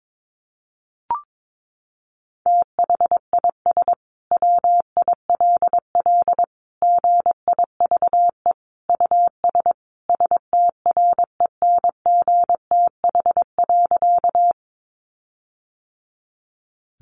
Morse code, heard as a keyed tone, about 22 wpm, 700 Hz. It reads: THIS WILL GIVE US STRENGTH.
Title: Sentences from Top 1300 Words 22wpm